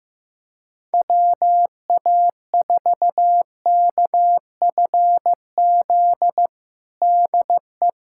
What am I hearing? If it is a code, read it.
Morse code, heard as a keyed tone, about 15 wpm, 700 Hz. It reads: WA4KFZ DE
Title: USA Callsign DE Callsign - 6 Characters - 15wpm